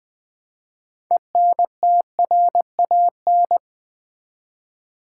Morse code, heard as a keyed tone, about 20 wpm, 700 Hz. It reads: ENTRAN
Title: Build A Word - 2 Letters At A Time - 8-Letter Words - 20wpm